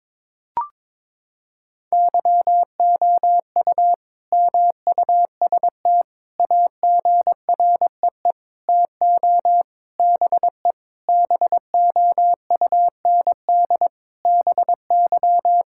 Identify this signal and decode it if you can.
Morse code, heard as a keyed tone, about 22 wpm, 700 Hz. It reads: YOU MUST AGREE TO BE BOUND BY
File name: Sentences with 9-Word Phrases - 22wpm